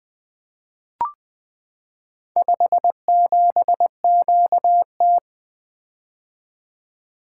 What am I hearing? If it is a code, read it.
Morse code, heard as a keyed tone, about 20 wpm, 700 Hz. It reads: 57QT
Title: Four Letter-Number Combinations - 20wpm